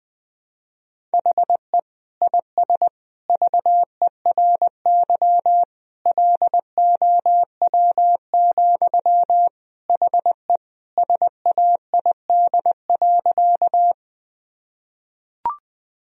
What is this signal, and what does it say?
Morse code, heard as a keyed tone, about 20 wpm, 700 Hz. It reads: HE IS VERY LOW, HE SAID.
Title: Sentences from Top 1000 Words - Review Part 1 - 20wpm